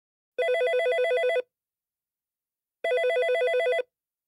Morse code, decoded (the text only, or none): none